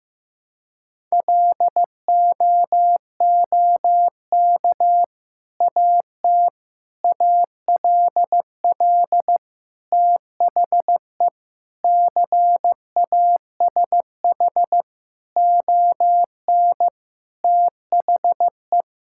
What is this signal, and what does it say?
Morse code, heard as a keyed tone, about 15 wpm, 700 Hz. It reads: LOOK AT ALL THE CASH ON THE
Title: Sentences from Top 1000 Words - Review Part 5 - 15wpm